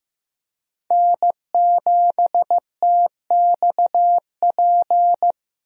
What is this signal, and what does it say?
Morse code, heard as a keyed tone, about 15 wpm, 700 Hz. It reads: N7TXP
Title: USA Call Signs - 5 Characters - Repeated 2x - 15wpm